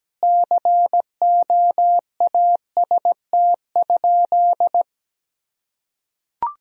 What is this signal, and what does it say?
Morse code, heard as a keyed tone, about 17 wpm, 700 Hz. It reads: COAST?
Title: Sentences from Top 1000 Words - Review Part 4 - 17wpm